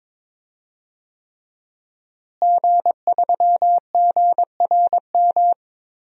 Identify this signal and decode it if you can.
Morse code, heard as a keyed tone, about 22 wpm, 700 Hz. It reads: G3GRM
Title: DXCC-100 Call Signs - 5 Characters - Repeated 3x - 22wpm